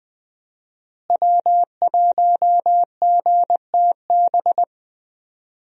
Morse code, heard as a keyed tone, about 21 wpm, 700 Hz. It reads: W1GTB
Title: USA Call Signs - 5 Characters - Repeated 2x - 20wpm